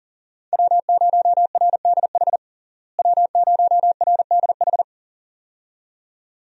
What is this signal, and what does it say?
Morse code, heard as a keyed tone, about 37 wpm, 700 Hz. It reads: W0RDH W0RDH
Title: USA Call Signs - 5 Characters - Repeated 2x - 40wpm